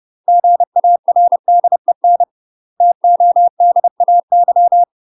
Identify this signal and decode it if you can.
Morse code, heard as a keyed tone, about 30 wpm, 700 Hz. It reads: GARDEN TODAY
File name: UK MASTER QSOs TRAINING/30WPM